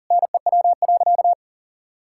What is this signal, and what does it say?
Morse code, heard as a keyed tone, about 40 wpm, 700 Hz. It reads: NEW.
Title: Sentences from Top 1600 Words 40wpm